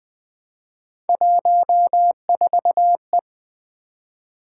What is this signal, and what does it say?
Morse code, heard as a keyed tone, about 20 wpm, 700 Hz. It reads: 14E
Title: Three Letter-Number Combinations - 20wpm